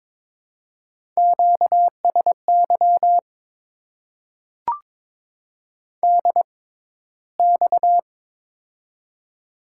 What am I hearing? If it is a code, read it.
Morse code, heard as a keyed tone, about 23 wpm, spaced as for 14 wpm, 700 Hz. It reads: QSY DX
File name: Top 90 QSO Elements - 6x Character Spacing - 22wpm